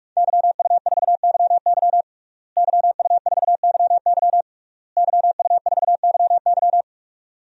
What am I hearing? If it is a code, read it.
Morse code, heard as a keyed tone, about 46 wpm, 700 Hz. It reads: YU4YY YU4YY YU4YY
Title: DXCC-100 Call Signs - 5 Characters - Repeated 3x - 45wpm